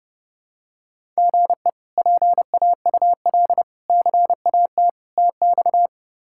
Morse code, heard as a keyed tone, about 30 wpm, 700 Hz. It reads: GE PAUL CAT TX
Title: K1USN SST QSO Practice - 30wpm